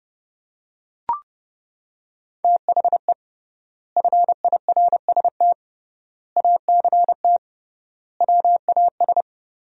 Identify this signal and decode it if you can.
Morse code, heard as a keyed tone, about 32 wpm, 700 Hz. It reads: THE FIRST ACT WAS
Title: 3x Word-Spacing - Sentences from Top 200 Words - 30wpm